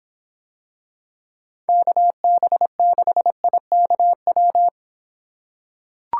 Morse code, heard as a keyed tone, about 25 wpm, 700 Hz. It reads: KB6IKW